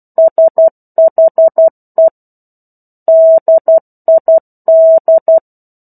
Morse code, heard as a keyed tone, about 12 wpm, 650 Hz. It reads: SHE DID
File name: 01-Chapter-01 12 WPM